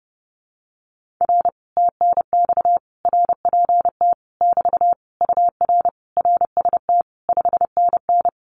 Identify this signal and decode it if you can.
Morse code, heard as a keyed tone, about 30 wpm, 700 Hz. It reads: R TNX RPT <BT> UR RST 5NN
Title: Common QSO Phrases Encore 30wpm